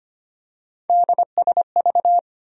Morse code, heard as a keyed tone, about 25 wpm, 700 Hz. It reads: DSV
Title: USA Call Sign Postfixes - 3 Letters 25wpm